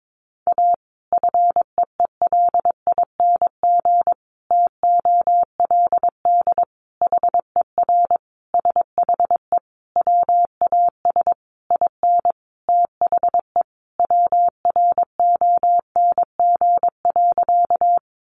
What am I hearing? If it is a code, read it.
Morse code, heard as a keyed tone, about 22 wpm, 700 Hz. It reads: A FEELING TOLD HER SHE WAS IN THE WRONG.
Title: Sentences from Top 1200 Words 22wpm